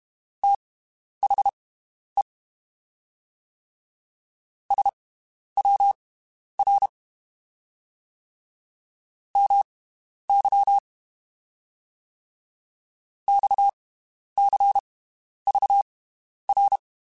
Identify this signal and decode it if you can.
Morse code, heard as a keyed tone, about 32 wpm, spaced as for 11 wpm, 800 Hz. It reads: THE SWR MY XCVR
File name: (spaced) / QSOs Spaced-mp3s / WD0FBT 1 @30 sp